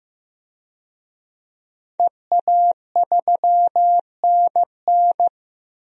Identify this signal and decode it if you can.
Morse code, heard as a keyed tone, about 15 wpm, 700 Hz. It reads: EA3NN
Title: DXCC-100 Call Signs - 5 Characters - Repeated 2x - 15wpm